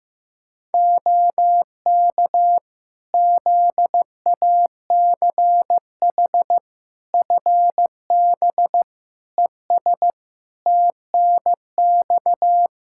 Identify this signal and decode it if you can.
Morse code, heard as a keyed tone, about 16 wpm, 700 Hz. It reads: OK ZACH FB ES TNX
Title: Practice QSO - Exchange 2 - Info - 15wpm